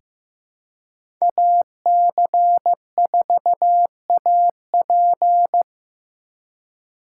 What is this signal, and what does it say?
Morse code, heard as a keyed tone, about 15 wpm, 700 Hz. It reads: AC4AP
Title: USA Call Signs - 5 Characters - 2x Character Spacing - 15wpm